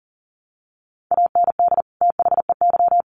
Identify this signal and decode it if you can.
Morse code, heard as a keyed tone, about 40 wpm, 700 Hz. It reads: AND THEY
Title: Sets of 2 Words 40wpm